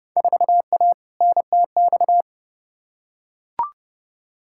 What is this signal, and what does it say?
Morse code, heard as a keyed tone, about 30 wpm, 700 Hz. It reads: 4A NTX
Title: Field Day Exchanges - 30wpm